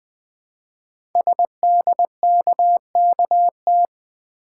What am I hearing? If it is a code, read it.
Morse code, heard as a keyed tone, about 20 wpm, 700 Hz. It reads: SDKKT
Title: Code Groups - 5 Letters - 20wpm